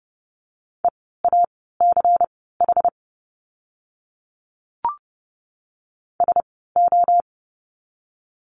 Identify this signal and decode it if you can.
Morse code, heard as a keyed tone, about 30 wpm, spaced as for 17 wpm, 700 Hz. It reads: EACH SO
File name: course / 30:10wpm / 230 Lesson 37f - Letter x - Words 30wpm